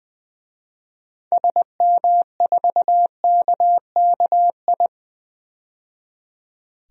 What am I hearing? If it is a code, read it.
Morse code, heard as a keyed tone, about 20 wpm, 700 Hz. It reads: SM4KKI
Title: DXCC-100 Call Signs - 6 Characters - 20wpm